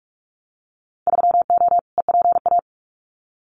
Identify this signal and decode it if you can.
Morse code, heard as a keyed tone, about 46 wpm, 700 Hz. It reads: 3O EPA
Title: Winter Field Day Exchanges 45wpm